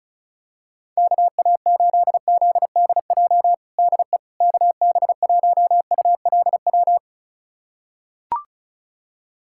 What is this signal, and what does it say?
Morse code, heard as a keyed tone, about 36 wpm, 700 Hz. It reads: KA8ZDJ DE KB1ULW
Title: USA Callsign DE Callsign 35wpm